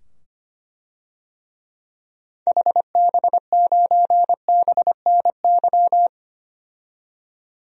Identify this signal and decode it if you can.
Morse code, heard as a keyed tone, about 25 wpm, 700 Hz. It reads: HB9BNY
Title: DXCC-100 Call Signs - 6 Characters - 25wpm